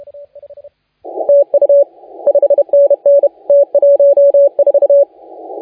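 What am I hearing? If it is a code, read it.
Morse code, heard as a keyed tone, about 31 wpm, 570 Hz. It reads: TU 5NN T14
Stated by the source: Morse contest audio recorded by LA4C during the SAC CW 2018 contest.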